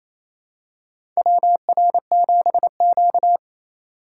Morse code, heard as a keyed tone, about 28 wpm, 700 Hz. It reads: WR7Q
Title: USA Call Signs - 4 Characters - 3x Character Spacing - 28wpm